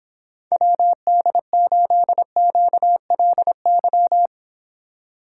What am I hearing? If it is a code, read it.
Morse code, heard as a keyed tone, about 31 wpm, 700 Hz. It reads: WD8QLY